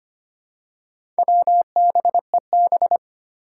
Morse code, heard as a keyed tone, about 25 wpm, 700 Hz. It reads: WBEB
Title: Four Challenging Letter-Number Combinations 25wpm